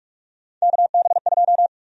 Morse code, heard as a keyed tone, about 46 wpm, 700 Hz. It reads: KB2
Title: USA Call Sign Prefixes - 45wpm